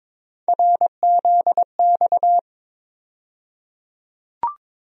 Morse code, heard as a keyed tone, about 22 wpm, 700 Hz. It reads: RZX
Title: USA Call Sign Postfixes - 3 Letters 22wpm